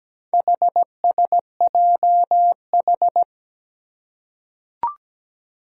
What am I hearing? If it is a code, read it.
Morse code, heard as a keyed tone, about 17 wpm, 700 Hz. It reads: HSJH
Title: Four Challenging Letter-Number Combinations 17wpm